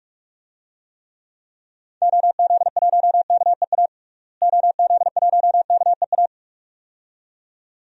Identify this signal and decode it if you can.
Morse code, heard as a keyed tone, about 47 wpm, 700 Hz. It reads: OZ1KEA OZ1KEA
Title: DXCC-100 Call Signs - 6 Characters - Repeated 2x - 45wpm